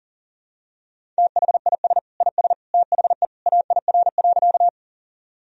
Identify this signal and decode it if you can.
Morse code, heard as a keyed tone, about 40 wpm, 700 Hz. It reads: THIS IS THE AIR.
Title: Sentences from Top 1000 Words - Review Part 1 - 40wpm